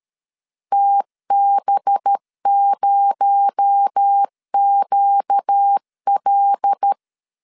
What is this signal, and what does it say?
Morse code, heard as a keyed tone, about 13 wpm, 800 Hz. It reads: TB0QL
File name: CW / DET / 1 6 1500C S A 12wpm 3440